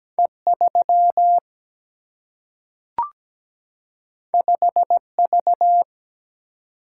Morse code, heard as a keyed tone, about 17 wpm, 700 Hz. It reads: E 3 5 V
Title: Single-Letter Single-Number 17wpm